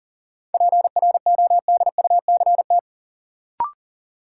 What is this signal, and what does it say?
Morse code, heard as a keyed tone, about 40 wpm, 700 Hz. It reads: PRODUCT
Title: Sentences from Top 1000 Words - Review Part 2 - New Word Drill - 40wpm